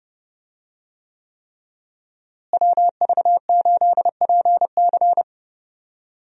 Morse code, heard as a keyed tone, about 29 wpm, 700 Hz. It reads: WV8PC
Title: USA Call Signs - 5 Characters - Repeated 3x - 30wpm